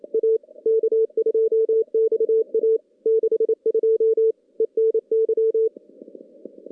Morse code, heard as a keyed tone, about 28 wpm, 440 Hz. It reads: A K2XA 62 ENY
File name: contests / CW-SS-k5tr- -2300